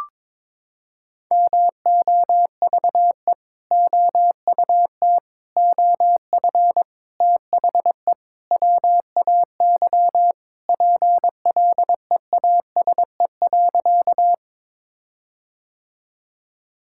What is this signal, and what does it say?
Morse code, heard as a keyed tone, about 21 wpm, 700 Hz. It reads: MOVE OUT OF THE WAY PLEASE.
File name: practice / 22wpm / Sentences with 4-Word Phrases - 22wpm